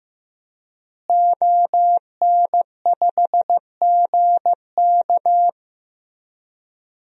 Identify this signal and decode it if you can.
Morse code, heard as a keyed tone, about 16 wpm, 700 Hz. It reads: ON5GK
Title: DXCC-100 Call Signs - 5 Characters - Repeated 3x - 15wpm